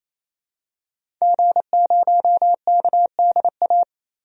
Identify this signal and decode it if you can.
Morse code, heard as a keyed tone, about 28 wpm, 700 Hz. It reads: G0KDA
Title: DXCC-100 Call Signs - 5 Characters - 28wpm